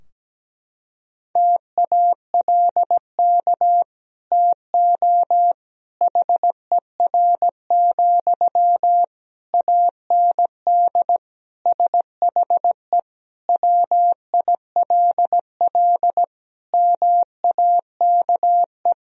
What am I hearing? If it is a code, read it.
Morse code, heard as a keyed tone, about 17 wpm, 700 Hz. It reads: TALK TO HER, AND SHE WILL MAKE
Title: Sentences from Top 1000 Words - Review Part 3 - 17wpm